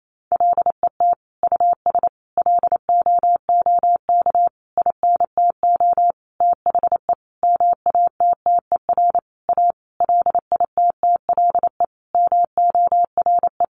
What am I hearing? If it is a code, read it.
Morse code, heard as a keyed tone, about 28 wpm, 700 Hz. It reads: LET US LOOK INTO THE MATTER A LITTLE MORE
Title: Sentences from Top 1400 Words 28wpm